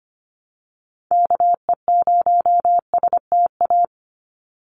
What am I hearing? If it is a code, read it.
Morse code, heard as a keyed tone, about 25 wpm, 700 Hz. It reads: KE0STA
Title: USA Call Signs 25wpm